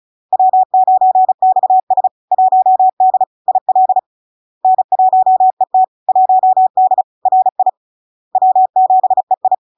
Morse code, heard as a keyed tone, about 35 wpm, 750 Hz. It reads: W9XS 1D IL N1ET 1D RI W7EI
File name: LARK-FD-2022-Log-in-CW-35wpm-750-Hz